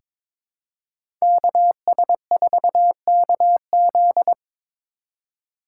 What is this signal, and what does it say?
Morse code, heard as a keyed tone, about 22 wpm, 700 Hz. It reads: KS4KZ
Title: USA Call Signs - 5 Characters - 4x Character Spacing - 22wpm